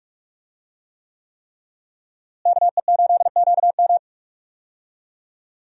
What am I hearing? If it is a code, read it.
Morse code, heard as a keyed tone, about 42 wpm, 700 Hz. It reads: KE8QM
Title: USA Call Signs - 5 Characters - Repeated 3x - 45wpm